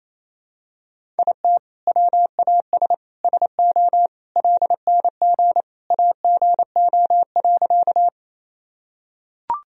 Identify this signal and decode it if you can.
Morse code, heard as a keyed tone, about 28 wpm, 700 Hz. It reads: IT WAS SO LNG AGO.
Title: Sentences from Top 500 Words and Abbreviations 28wpm